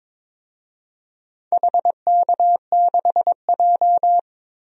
Morse code, heard as a keyed tone, about 21 wpm, 700 Hz. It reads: HK6J
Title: DXCC-100 Call Signs - 4 Characters - 22wpm